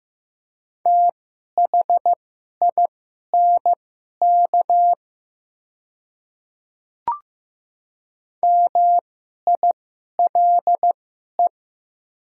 Morse code, heard as a keyed tone, about 16 wpm, spaced as for 11 wpm, 700 Hz. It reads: THINK MILE
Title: 2x Character Spacing - Top 300 Words - 15wpm